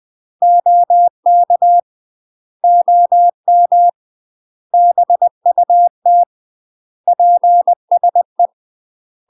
Morse code, heard as a keyed tone, about 20 wpm, 700 Hz. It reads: OK OM BUT PSE
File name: UK MASTER QSOs TRAINING / 20WPM / UK MASTER QSOs 0063